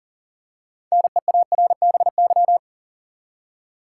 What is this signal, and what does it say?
Morse code, heard as a keyed tone, about 40 wpm, 700 Hz. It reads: NEARBY
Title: Six Letter Words - 40wpm